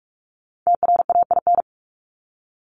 Morse code, heard as a keyed tone, about 44 wpm, 700 Hz. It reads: TRAIN
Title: Top 1751-2000 Words 45wpm